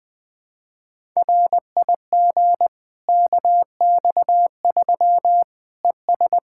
Morse code, heard as a keyed tone, about 20 wpm, 700 Hz. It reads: RIG KX3 ES
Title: Practice QSO - Exchange 2 - Info - 20wpm